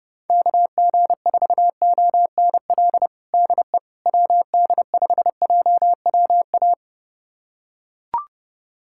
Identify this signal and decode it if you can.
Morse code, heard as a keyed tone, about 30 wpm, 700 Hz. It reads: KG4ONL DE WD5JWA
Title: USA Callsign DE Callsign - 6 Characters - 30wpm